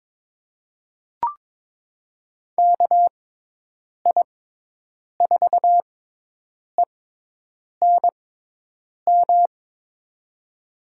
Morse code, heard as a keyed tone, about 22 wpm, spaced as for 8 wpm, 700 Hz. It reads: KI4ENM
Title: USA Call Signs - 6x Character Spacing - 22wpm